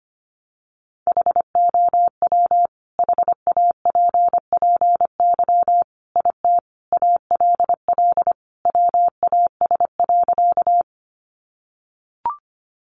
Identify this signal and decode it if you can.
Morse code, heard as a keyed tone, about 25 wpm, 700 Hz. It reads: HOW HAPPY IT ALL WAS.
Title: Sentences from Top 1000 Words 25wpm